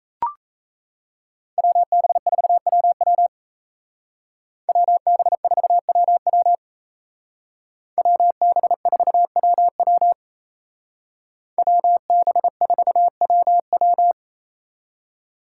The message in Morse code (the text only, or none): WB4WW WB4WW WB4WW WB4WW